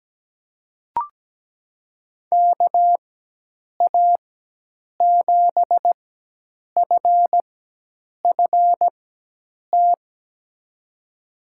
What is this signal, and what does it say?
Morse code, heard as a keyed tone, about 17 wpm, spaced as for 8 wpm, 700 Hz. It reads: KA7FFT
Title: USA Call Signs - 6 Characters - 4x Character Spacing - 17wpm